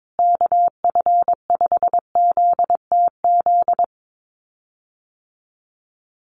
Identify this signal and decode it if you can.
Morse code, heard as a keyed tone, about 22 wpm, 700 Hz. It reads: KF5ZTZ
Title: USA Call Signs Repeated 3x 22wpm